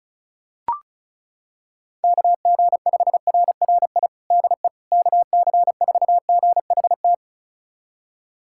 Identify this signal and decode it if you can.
Morse code, heard as a keyed tone, about 36 wpm, 700 Hz. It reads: KG5RRI DE KC4GHT
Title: USA Callsign DE Callsign - 6 Characters - 35wpm